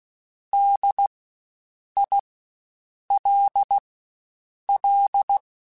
Morse code, heard as a keyed tone, about 16 wpm, spaced as for 7 wpm, 800 Hz. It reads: DILL
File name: Spaced Alphas / Spaced Alphas mp3s / Ds sp @15